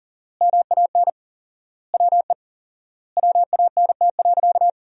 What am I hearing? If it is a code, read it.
Morse code, heard as a keyed tone, about 40 wpm, 700 Hz. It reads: MAN WE WANT.
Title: 4x Word-Spacing - Sentences from Top 200 Words - 40wpm